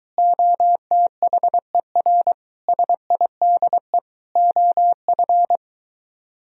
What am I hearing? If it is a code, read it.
Morse code, heard as a keyed tone, about 22 wpm, 700 Hz. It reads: OTHER SIDE OF